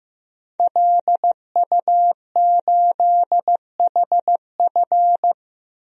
Morse code, heard as a keyed tone, about 15 wpm, 700 Hz. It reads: LU8HF
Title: DXCC-100 Call Signs - 5 Characters - 15wpm